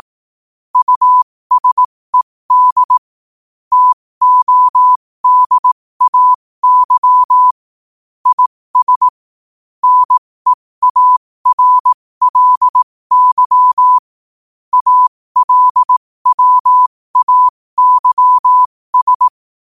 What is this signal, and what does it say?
Morse code, heard as a keyed tone, about 18 wpm, spaced as for 15 wpm, 1000 Hz. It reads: USED TODAY IS NEARLY ALWAYS
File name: qotdfiles / 15wpm